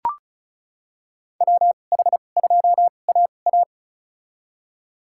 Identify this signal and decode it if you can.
Morse code, heard as a keyed tone, about 34 wpm, spaced as for 25 wpm, 700 Hz. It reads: WH2AA
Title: USA Call Signs - 5 Characters - 2x Character Spacing - 35wpm